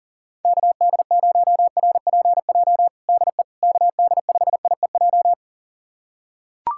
Morse code, heard as a keyed tone, about 40 wpm, 700 Hz. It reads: KD0RPJ DE KD5IEJ
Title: USA Callsign DE Callsign - 6 Characters - 40wpm